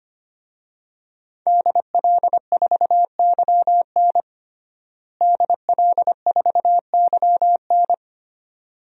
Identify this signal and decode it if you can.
Morse code, heard as a keyed tone, about 25 wpm, 700 Hz. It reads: DL4YN DL4YN
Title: DXCC-100 Call Signs - 5 Characters - Repeated 2x - 25wpm